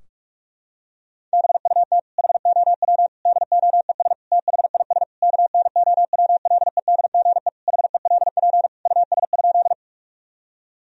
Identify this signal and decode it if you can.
Morse code, heard as a keyed tone, about 46 wpm, 700 Hz. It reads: BUT HOW DOES THIS KNOWLEDGE HELP US?
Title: Sentences from Top 1000 Words - Review Part 3 - 45wpm